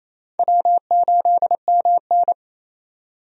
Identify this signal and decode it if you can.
Morse code, heard as a keyed tone, about 27 wpm, 700 Hz. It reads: W8MN
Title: USA Call Signs - 4 Characters - 28wpm